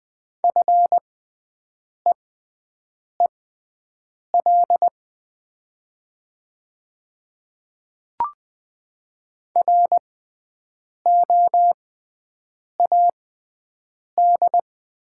Morse code, heard as a keyed tone, about 20 wpm, spaced as for 7 wpm, 700 Hz. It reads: FEEL ROAD
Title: 6x Character Spacing - Top 500 Words - 20wpm